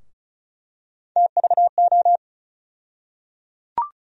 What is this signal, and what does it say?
Morse code, heard as a keyed tone, about 34 wpm, 700 Hz. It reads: TVO
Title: USA Call Sign Postfixes - 3 Letters 35wpm